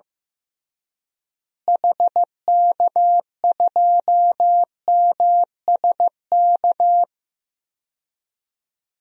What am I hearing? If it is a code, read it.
Morse code, heard as a keyed tone, about 15 wpm, 700 Hz. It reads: HK2MSK
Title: DXCC-100 Call Signs - 6 Characters - Repeated 3x - 15wpm